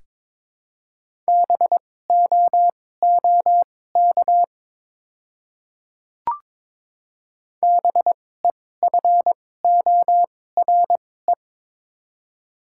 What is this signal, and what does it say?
Morse code, heard as a keyed tone, about 22 wpm, spaced as for 16 wpm, 700 Hz. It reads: BOOK BEFORE
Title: 2x Character Spacing - Top 500 Words - 22wpm